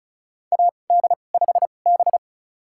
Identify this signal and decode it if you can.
Morse code, heard as a keyed tone, about 34 wpm, spaced as for 25 wpm, 700 Hz. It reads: AD5B
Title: USA Call Signs - 4 Characters - 2x Character Spacing - 35wpm